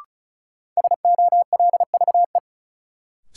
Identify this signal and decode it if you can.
Morse code, heard as a keyed tone, about 34 wpm, 700 Hz. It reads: SOLVE
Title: Sentences from Top 1000 Words - New Word Drill - 35wpm